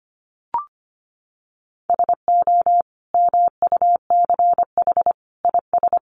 Morse code, heard as a keyed tone, about 25 wpm, 700 Hz. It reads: SO MUCH IS
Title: Sentences from Top 1200 Words 25wpm